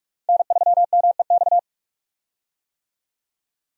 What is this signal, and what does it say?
Morse code, heard as a keyed tone, about 46 wpm, 700 Hz. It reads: N3MEX
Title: USA Call Signs - 5 Characters - 45wpm